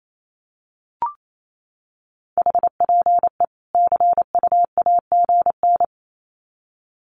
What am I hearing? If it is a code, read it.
Morse code, heard as a keyed tone, about 28 wpm, 700 Hz. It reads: HPE CUAGN
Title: Common QSO Phrases 28wpm